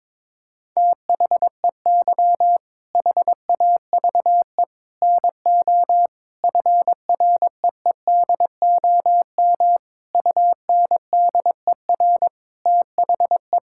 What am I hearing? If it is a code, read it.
Morse code, heard as a keyed tone, about 22 wpm, 700 Hz. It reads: THEY HAVE NO FREEDOM UNDER THE
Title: Sentences from Top 1000 Words - Review Part 5 - 22wpm